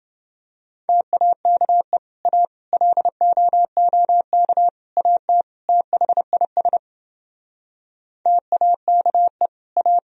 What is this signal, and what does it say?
Morse code, heard as a keyed tone, about 30 wpm, 700 Hz. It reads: TAKE A LOOK AT THIS TAKE A